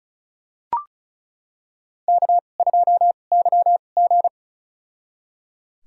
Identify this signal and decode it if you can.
Morse code, heard as a keyed tone, about 34 wpm, spaced as for 25 wpm, 700 Hz. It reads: K2YG
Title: USA Call Signs - 4 Characters - 2x Character Spacing - 35wpm